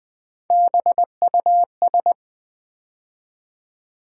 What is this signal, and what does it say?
Morse code, heard as a keyed tone, about 20 wpm, 700 Hz. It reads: BUS
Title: Build A Word - 3 Letters At A Time - 8-Letter Words - 20wpm